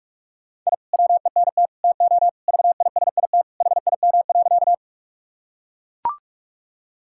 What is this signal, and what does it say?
Morse code, heard as a keyed tone, about 41 wpm, 700 Hz. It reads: I WENT TO VISIT HIM.
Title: Sentences from Top 1000 Words - Review Part 6 - 45wpm